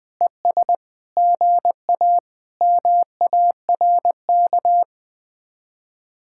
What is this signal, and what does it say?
Morse code, heard as a keyed tone, about 20 wpm, 700 Hz. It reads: ES GA MARK
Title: Speed-Racing Common QSO Phrases Encore - 20wpm